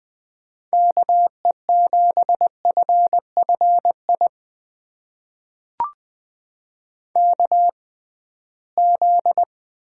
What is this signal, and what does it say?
Morse code, heard as a keyed tone, about 20 wpm, spaced as for 15 wpm, 700 Hz. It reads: KE7FFI KZ
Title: USA Call Signs - 6x Character Spacing - 20wpm